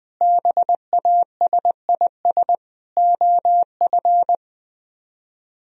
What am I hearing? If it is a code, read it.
Morse code, heard as a keyed tone, about 20 wpm, 700 Hz. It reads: BASIS OF
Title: Speed-Racing - Sets of 4 Words 17wpm